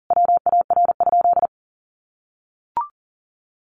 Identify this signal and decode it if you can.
Morse code, heard as a keyed tone, about 38 wpm, 700 Hz. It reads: WAR?
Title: Sentences from Top 1400 Words 40wpm